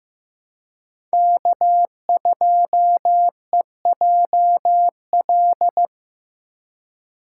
Morse code, heard as a keyed tone, about 18 wpm, 700 Hz. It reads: K2EJL
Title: Speed Racing - USA 1x3 Call Signs 15wpm